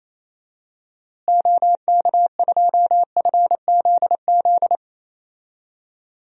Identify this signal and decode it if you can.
Morse code, heard as a keyed tone, about 28 wpm, 700 Hz. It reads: OK2FZZ
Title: DXCC-100 Call Signs - 6 Characters - Repeated 3x - 28wpm